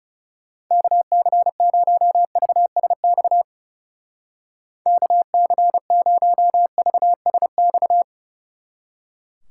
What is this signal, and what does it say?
Morse code, heard as a keyed tone, about 37 wpm, 700 Hz. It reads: KC0VSX KC0VSX
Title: Speed Racing - USA 2x3 Call Signs 30wpm